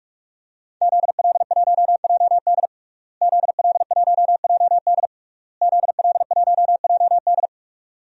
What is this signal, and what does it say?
Morse code, heard as a keyed tone, about 41 wpm, 700 Hz. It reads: ZL1JD ZL1JD ZL1JD
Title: DXCC-100 Call Signs - 5 Characters - Repeated 3x - 45wpm